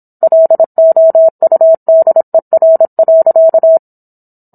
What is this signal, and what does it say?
Morse code, heard as a keyed tone, about 27 wpm, 650 Hz. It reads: LOUDER.